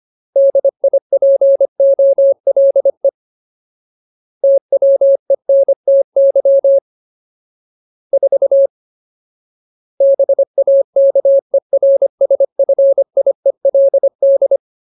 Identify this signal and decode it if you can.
Morse code, heard as a keyed tone, about 25 wpm, 550 Hz. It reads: DIPOLE TWENTY 4 BAKERSFIELD
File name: Ultimate Word List / 25WPM Ultimate